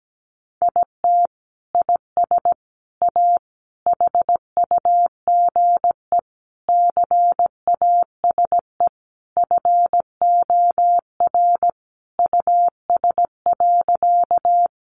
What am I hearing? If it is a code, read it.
Morse code, heard as a keyed tone, about 17 wpm, 700 Hz. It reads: IT IS A HUGE CASE FOR US.
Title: Sentences from Top 1100 Words 17wpm